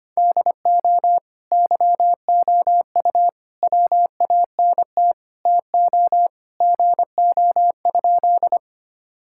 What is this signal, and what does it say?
Morse code, heard as a keyed tone, about 25 wpm, 700 Hz. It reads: DO YOU WANT TO GO?
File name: practice / 25wpm / Sentences with 3-Word Phrases - 25wpm